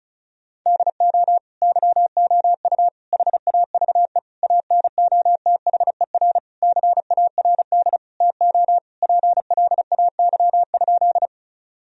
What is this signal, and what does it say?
Morse code, heard as a keyed tone, about 34 wpm, 700 Hz. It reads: DO YOU HAVE ANOTHER CARD TO PLAY?
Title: Sentences from Top 1000 Words - Review Part 4 - 35wpm